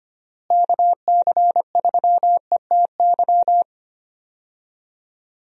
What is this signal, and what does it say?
Morse code, heard as a keyed tone, about 25 wpm, 700 Hz. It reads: KC3ETY
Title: USA Call Signs - 6 Characters - Repeated 2x - 25wpm